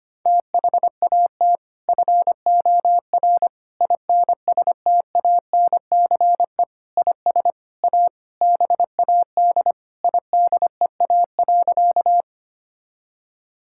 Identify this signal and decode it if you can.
Morse code, heard as a keyed tone, about 25 wpm, 700 Hz. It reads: THAT FOR INSTANCE IS A BAD IDEA.
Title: Sentences from Top 1000 Words - Review Part 5 - 25wpm